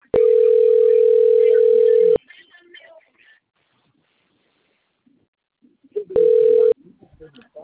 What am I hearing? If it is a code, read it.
Ringback tone (North America), 440 + 480 Hz.